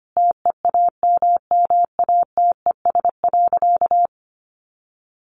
Morse code, heard as a keyed tone, about 25 wpm, 700 Hz. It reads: TEAMMATES.
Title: Easy Phrases 25wpm